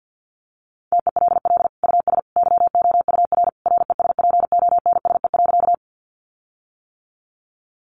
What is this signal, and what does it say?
Morse code, heard as a keyed tone, about 50 wpm, 700 Hz. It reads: TELL US YOUR RESPONSE.
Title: Sentences from Top 1400 Words 50wpm